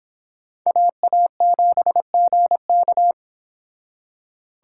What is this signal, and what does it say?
Morse code, heard as a keyed tone, about 25 wpm, 700 Hz. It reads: AA7GK